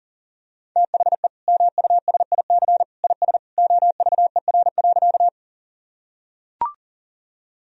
Morse code, heard as a keyed tone, about 40 wpm, 700 Hz. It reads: THE MUSIC IS OVER.
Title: Sentences from Top 1000 Words - Review Part 1 - 40wpm